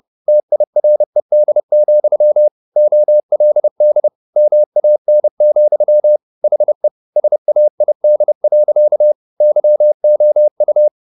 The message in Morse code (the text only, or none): TIRED, OLD MAN, HE SAID. YOU